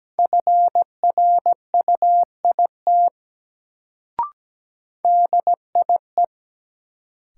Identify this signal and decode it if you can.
Morse code, heard as a keyed tone, about 17 wpm, 700 Hz. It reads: FRUIT DIE
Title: Sentences from Top 1300 Words - New Word Drill Alt - 17wpm